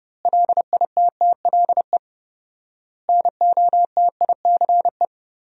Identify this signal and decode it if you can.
Morse code, heard as a keyed tone, about 30 wpm, 700 Hz. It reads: LITTLE NOTICE
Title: 4x Word-Spacing - Sentences from Top 400 Words - 30wpm